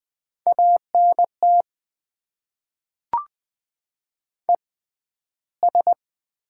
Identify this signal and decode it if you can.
Morse code, heard as a keyed tone, about 20 wpm, spaced as for 12 wpm, 700 Hz. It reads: ANT ES
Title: Top 20 QSO Elements - 6x Character Spacing - 20wpm